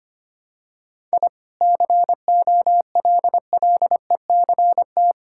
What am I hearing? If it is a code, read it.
Morse code, heard as a keyed tone, about 25 wpm, 690 Hz. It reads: I COLLECT
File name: Easy Sentences with Hints - 25wpm